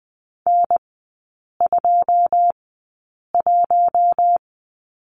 Morse code, heard as a keyed tone, about 20 wpm, spaced as for 8 wpm, 700 Hz. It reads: N21
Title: 093 Lesson 16e - Letter c - Three Characters 20wpm